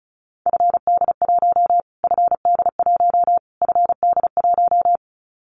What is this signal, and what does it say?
Morse code, heard as a keyed tone, about 34 wpm, 700 Hz. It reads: FD1 FD1 FD1
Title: Callsign Prefixes for Top 25 DXCC - Repeated 3x - 35wpm